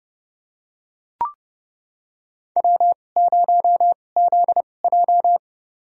Morse code, heard as a keyed tone, about 31 wpm, spaced as for 22 wpm, 700 Hz. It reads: W0ZJ
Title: USA Call Signs - 4 Characters - 2x Character Spacing - 30wpm